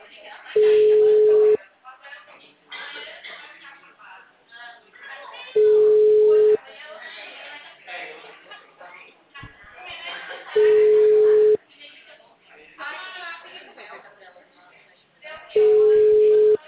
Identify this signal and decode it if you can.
Ringback tone (Europe, 425 Hz), with speech.
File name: out- -2011- - - .14